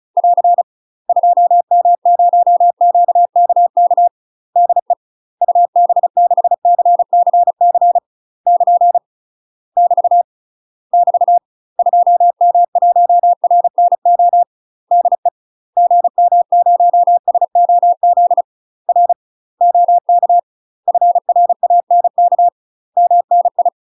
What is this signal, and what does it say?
Morse code, heard as a keyed tone, about 35 wpm, 700 Hz. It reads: <AR> 2M0QKK DE UB6CCC <KN> <BT> <BT> 2M1RNO DE GM0SOZ R OK FRANK MNI
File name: UK MASTER QSOs TRAINING / 35WPM / UK MASTER QSOs 0033